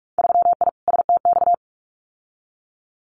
Morse code, heard as a keyed tone, about 46 wpm, 700 Hz. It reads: 3I STX
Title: Winter Field Day Exchanges Encore 45wpm